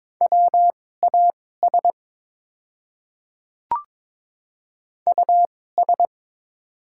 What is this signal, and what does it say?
Morse code, heard as a keyed tone, about 22 wpm, spaced as for 16 wpm, 700 Hz. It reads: WAS US
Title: 2x Character Spacing - Top 200 Words - 22wpm